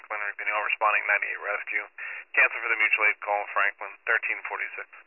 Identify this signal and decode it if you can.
Emergency Alert System (SAME) tones, heard as harsh buzzing bursts, with speech. It is heard from 1.0 to 4.4 s.